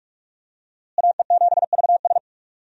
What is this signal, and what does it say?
Morse code, heard as a keyed tone, about 43 wpm, 700 Hz. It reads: AE7VS
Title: USA Call Signs - 5 Characters - Repeated 2x - 45wpm